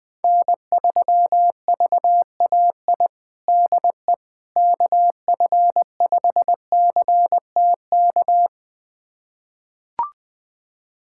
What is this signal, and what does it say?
Morse code, heard as a keyed tone, about 20 wpm, 700 Hz. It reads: N3VAI DE KF5CTK
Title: USA Callsign DE Callsign 20wpm